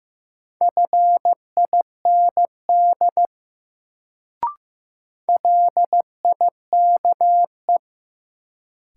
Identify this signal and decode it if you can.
Morse code, heard as a keyed tone, about 15 wpm, 700 Hz. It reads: FIND LIKE
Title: Sentences from Top 100 Words - New Word Drill - 15wpm